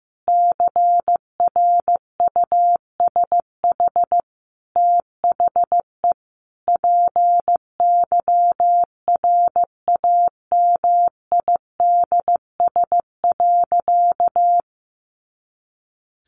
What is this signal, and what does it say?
Morse code, heard as a keyed tone, about 15 wpm, 700 Hz. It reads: CRUSH THE PYRAMIDS.
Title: A Sound Of Thunder 15wpm